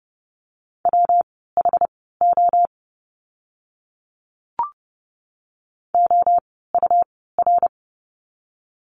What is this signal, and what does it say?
Morse code, heard as a keyed tone, about 30 wpm, spaced as for 17 wpm, 700 Hz. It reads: WHO OUR
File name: course / 30:10wpm / 115 Lesson 19f - Letter w - Words 30wpm